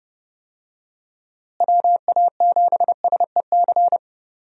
Morse code, heard as a keyed tone, about 29 wpm, 700 Hz. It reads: WA7SEC
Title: USA Call Signs - 6 Characters - Repeated 3x - 30wpm